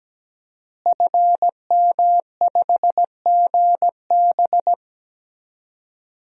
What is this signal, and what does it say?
Morse code, heard as a keyed tone, about 17 wpm, 700 Hz. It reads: FM5GB
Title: DXCC-100 Call Signs - 5 Characters - Repeated 3x - 17wpm